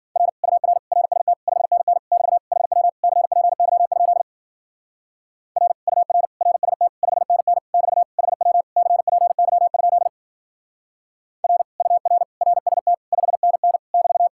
Morse code, heard as a keyed tone, about 51 wpm, 700 Hz. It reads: R UR RST 5NN <BT> HW CPY? R UR RST 5NN <BT> HW CPY? R UR RST 5NN <BT>